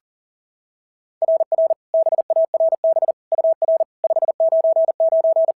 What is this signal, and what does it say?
Morse code, heard as a keyed tone, about 40 wpm, 690 Hz. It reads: RR BARB UR 599
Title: SOTA QSO Practice 40wpm